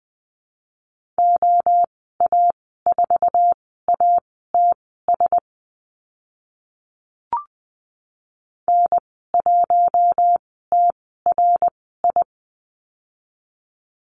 Morse code, heard as a keyed tone, about 20 wpm, spaced as for 14 wpm, 700 Hz. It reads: OA4ATS N1TRI
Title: 062 Lesson 11g - Letter h - Call Signs 20wpm